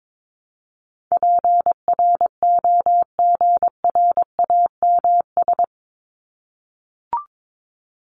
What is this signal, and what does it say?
Morse code, heard as a keyed tone, about 22 wpm, 700 Hz. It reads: PROGRAMS
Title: ES:S Suffix - Top 100 Words with Base Words 22wpm